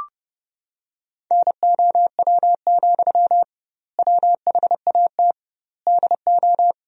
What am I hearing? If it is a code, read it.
Morse code, heard as a keyed tone, about 30 wpm, 700 Hz. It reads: NOW, WHAT DO
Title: 2x Word-Spacing - Sentences from Top 200 Words - 30wpm